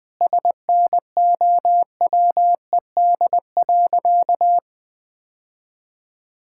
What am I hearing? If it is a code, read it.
Morse code, heard as a keyed tone, about 21 wpm, 700 Hz. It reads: SNOWED.
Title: Sentences with 5-Word Phrases - 20wpm